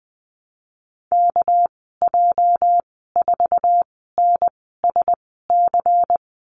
Morse code, heard as a keyed tone, about 20 wpm, spaced as for 14 wpm, 700 Hz. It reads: KJ4NSC
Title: 207 Lesson 33g - Letter j - Call Signs 20wpm